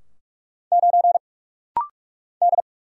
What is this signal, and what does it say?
Morse code, heard as a keyed tone, about 46 wpm, 700 Hz. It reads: 9 D
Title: Single Letter-Number Alt - Rapid-Fire 45wpm